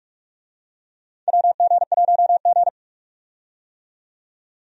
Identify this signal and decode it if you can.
Morse code, heard as a keyed tone, about 46 wpm, 700 Hz. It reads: WG1G
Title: USA Call Signs - 4 Characters - 45wpm